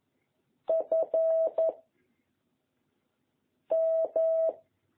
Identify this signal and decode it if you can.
Morse code, heard as a keyed tone, about 11 wpm, 660 Hz. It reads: F M
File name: jh8cbh / 2021cw